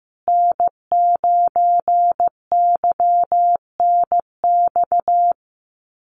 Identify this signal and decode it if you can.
Morse code, heard as a keyed tone, about 15 wpm, 700 Hz. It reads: N9YNX
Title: USA Call Signs Repeated 2x 15wpm